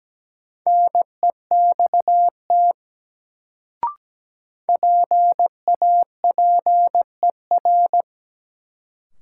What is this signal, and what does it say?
Morse code, heard as a keyed tone, about 17 wpm, 700 Hz. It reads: NEXT PAPER
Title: Sentences from Top 300 Words - New Word Drill - 17wpm